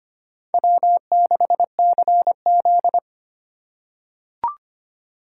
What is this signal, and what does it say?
Morse code, heard as a keyed tone, about 25 wpm, 700 Hz. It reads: W6CZ
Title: USA Call Signs - 4 Characters - 2x Character Spacing - 25wpm